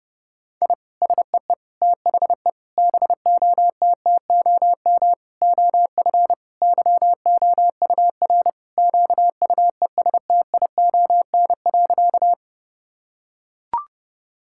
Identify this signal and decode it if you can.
Morse code, heard as a keyed tone, about 30 wpm, 700 Hz. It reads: I SEE THE BOTTOM OF YOUR QUESTION.
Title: Sentences from Top 1000 Words - Review Part 6 - 30wpm